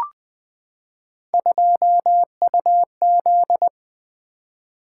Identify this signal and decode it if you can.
Morse code, heard as a keyed tone, about 20 wpm, 700 Hz. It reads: 2UZ
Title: Three Letter-Number Combinations - 20wpm